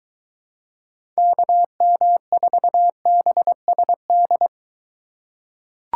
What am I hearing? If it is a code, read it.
Morse code, heard as a keyed tone, about 21 wpm, 700 Hz. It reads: KM4BSD